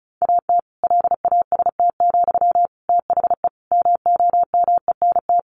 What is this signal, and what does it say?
Morse code, heard as a keyed tone, about 34 wpm, 700 Hz. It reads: AT LAST, THE MOMENT
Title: Sentences from Top 1500 Words 35wpm